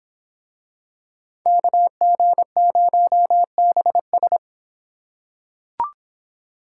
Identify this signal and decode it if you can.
Morse code, heard as a keyed tone, about 23 wpm, 700 Hz. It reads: KG0BS